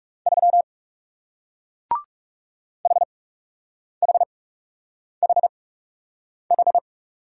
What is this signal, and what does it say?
Morse code, heard as a keyed tone, about 41 wpm, spaced as for 8 wpm, 700 Hz. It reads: 3 HHHH